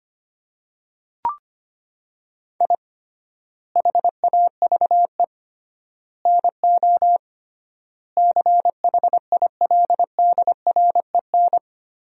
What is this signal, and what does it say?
Morse code, heard as a keyed tone, about 25 wpm, 700 Hz. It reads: I HAVE NO CHILDREN
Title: 3x Word-Spacing - Sentences from Top 400 Words - 25wpm